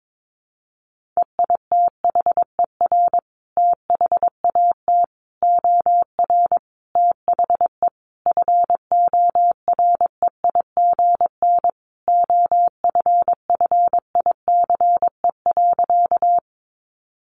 Morse code, heard as a keyed tone, about 22 wpm, 700 Hz. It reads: EITHER THAT OR THE FOREIGN OFFICE.
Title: Sentences from Top 1400 Words 22wpm